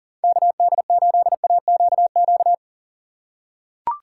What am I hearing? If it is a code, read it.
Morse code, heard as a keyed tone, about 40 wpm, 700 Hz. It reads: KD8AQQ
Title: USA Call Signs - 6 Characters - 4x Character Spacing - 40wpm